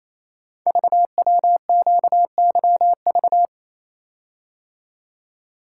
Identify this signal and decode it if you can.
Morse code, heard as a keyed tone, about 27 wpm, 700 Hz. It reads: VWQYV
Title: Code Groups - 5 Letters - 28wpm